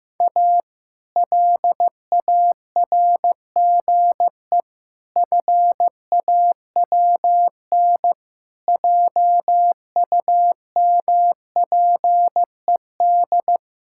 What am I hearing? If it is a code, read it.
Morse code, heard as a keyed tone, about 14 wpm, 700 Hz. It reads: A LARGE FAWN JUMPED
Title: Pangram Challenge v2 - 15wpm